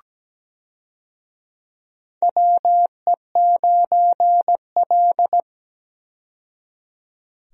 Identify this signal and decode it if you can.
Morse code, heard as a keyed tone, about 17 wpm, 700 Hz. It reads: WE9L
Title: USA Call Signs - 4 Characters - 17wpm